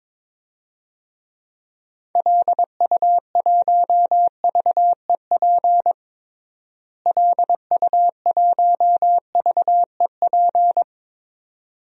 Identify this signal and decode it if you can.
Morse code, heard as a keyed tone, about 22 wpm, 700 Hz. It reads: LU1VEP LU1VEP
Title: DXCC-100 Call Signs - 6 Characters - Repeated 2x - 22wpm